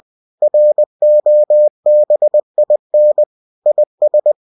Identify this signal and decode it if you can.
Morse code, heard as a keyed tone, about 20 wpm, 600 Hz. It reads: ROBIN IS
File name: cw-books / winnie-the-pooh / winnie-20wpm-page-0107